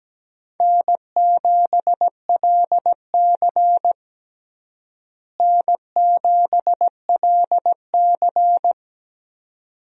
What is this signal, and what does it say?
Morse code, heard as a keyed tone, about 17 wpm, 700 Hz. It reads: N7LC N7LC
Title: USA Call Signs - 4 Characters - Repeated 2x - 17wpm